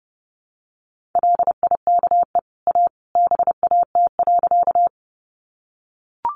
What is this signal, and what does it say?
Morse code, heard as a keyed tone, about 30 wpm, 700 Hz. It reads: LIKE A BAT.
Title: Sentences from Top 1500 Words 30wpm